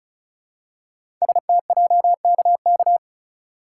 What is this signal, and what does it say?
Morse code, heard as a keyed tone, about 36 wpm, 700 Hz. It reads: STJKK
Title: Code Groups - 5 Letters - 35wpm